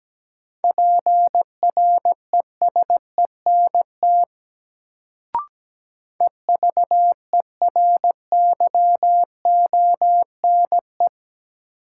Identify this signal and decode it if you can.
Morse code, heard as a keyed tone, about 17 wpm, 700 Hz. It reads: PRESENT EVERYONE
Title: Sentences from Top 800 Words - New Word Drill - 17wpm